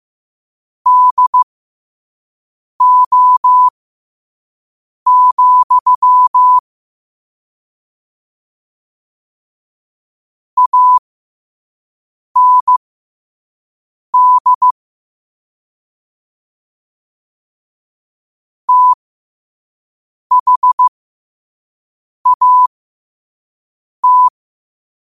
Morse code, heard as a keyed tone, about 15 wpm, spaced as for 5 wpm, 1000 Hz. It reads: DO, AND THAT
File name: qotdfiles / 5wpm